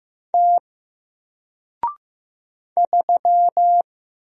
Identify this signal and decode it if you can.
Morse code, heard as a keyed tone, about 15 wpm, 700 Hz. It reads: T 3
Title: Single Letter-Number 15wpm